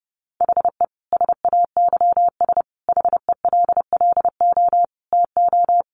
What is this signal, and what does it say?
Morse code, heard as a keyed tone, about 30 wpm, 700 Hz. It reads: HE SAYS HELLO TO
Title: Additional Easy Phrases 30wpm